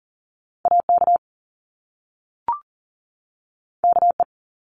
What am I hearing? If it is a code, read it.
Morse code, heard as a keyed tone, about 40 wpm, 700 Hz. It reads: AK KE
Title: Top Two Letter Combinations in Words 40wpm